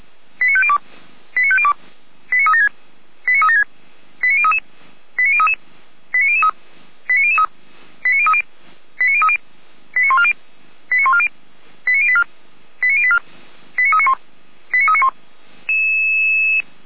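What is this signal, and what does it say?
Five-tone selective call (ZVEI), heard as quick runs of five tones, tones of about 70 ms. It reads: 89502 89502 89376 89376 89033 89033 89003 89003 89030 89030 89140 89140 89084 89084 89391 89391
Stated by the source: ZVEI five-tone call 89071 received by SDR.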